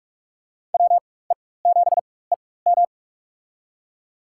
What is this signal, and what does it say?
Morse code, heard as a keyed tone, about 41 wpm, spaced as for 20 wpm, 700 Hz. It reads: WE7EM
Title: USA Call Signs - 4x Character Spacing - 45wpm